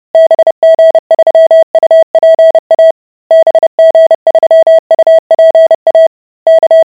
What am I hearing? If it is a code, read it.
Morse code, heard as a keyed tone, about 30 wpm, 650 Hz. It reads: BG3UPA BG3UPA K